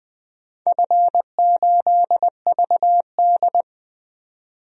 Morse code, heard as a keyed tone, about 21 wpm, 700 Hz. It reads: F8VD
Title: DXCC-100 Call Signs - 4 Characters - 20wpm